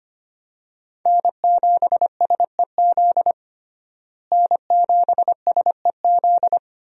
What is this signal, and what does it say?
Morse code, heard as a keyed tone, about 25 wpm, 700 Hz. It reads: N7SEZ N7SEZ
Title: USA Call Signs - 5 Characters - Repeated 2x - 25wpm